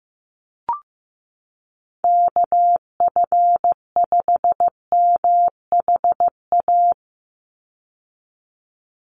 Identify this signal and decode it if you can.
Morse code, heard as a keyed tone, about 15 wpm, 700 Hz. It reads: KF5MHA
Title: USA Call Signs 15wpm